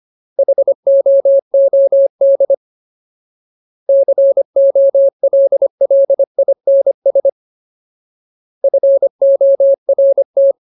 Morse code, heard as a keyed tone, about 25 wpm, 550 Hz. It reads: HOOD COLLINS FORT
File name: Ultimate Word List / 25WPM Ultimate